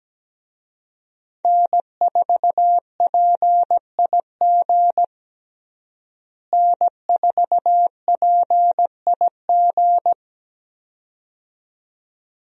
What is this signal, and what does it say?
Morse code, heard as a keyed tone, about 17 wpm, 700 Hz. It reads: N4PIG N4PIG
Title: USA Call Signs - 5 Characters - Repeated 2x - 17wpm